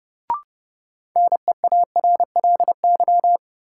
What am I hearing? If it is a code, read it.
Morse code, heard as a keyed tone, about 30 wpm, 700 Hz. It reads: NEARLY
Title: Sentences from Top 1300 Words - New Word Drill Alt - 30wpm